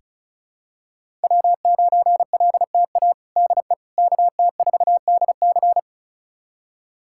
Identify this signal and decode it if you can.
Morse code, heard as a keyed tone, about 36 wpm, 700 Hz. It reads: W9LTA DE KT4DC
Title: USA Callsign DE Callsign - 5 Characters - 35wpm